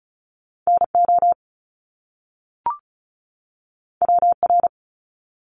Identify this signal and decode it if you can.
Morse code, heard as a keyed tone, about 36 wpm, 700 Hz. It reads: NO WR
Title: Top Two Letter Combinations in Words 35wpm